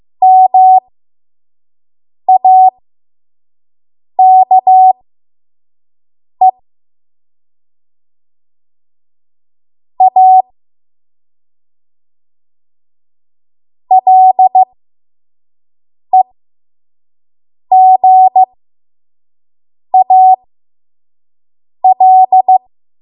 Morse code, heard as a keyed tone, about 15 wpm, spaced as for 5 wpm, 750 Hz. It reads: MAKE A LEGAL
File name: Morse / 05WPM